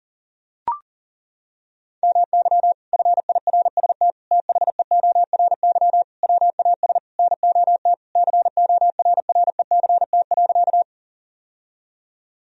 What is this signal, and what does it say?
Morse code, heard as a keyed tone, about 37 wpm, 700 Hz. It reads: MY FIRST THEORY WAS NOT CORRECT.
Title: Sentences from Top 1000 Words - Review Part 3 - 40wpm